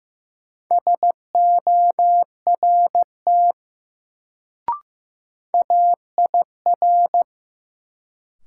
Morse code, heard as a keyed tone, about 15 wpm, 700 Hz. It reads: SORT AIR
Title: Sets of 2 Words - New Word Drill - 15wpm